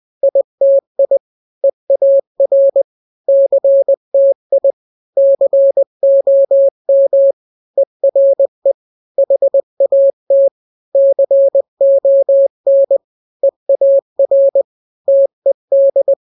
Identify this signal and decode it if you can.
Morse code, heard as a keyed tone, about 20 wpm, 550 Hz. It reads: ITI EAR CTI COM ERE HAT CON EAR TED
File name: n-grams / 3-grams 20 18 WPM